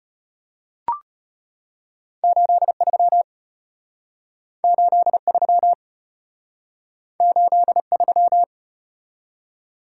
Morse code, heard as a keyed tone, about 33 wpm, 700 Hz. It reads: 83 83 83